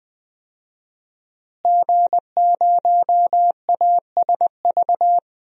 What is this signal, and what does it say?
Morse code, heard as a keyed tone, about 20 wpm, 700 Hz. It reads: G0ASV
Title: DXCC-100 Call Signs - 5 Characters - Repeated 3x - 20wpm